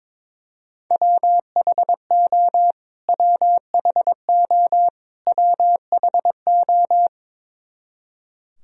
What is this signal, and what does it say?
Morse code, heard as a keyed tone, about 22 wpm, 700 Hz. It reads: WHO WHO WHO
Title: Top 100 Words Repeated 22wpm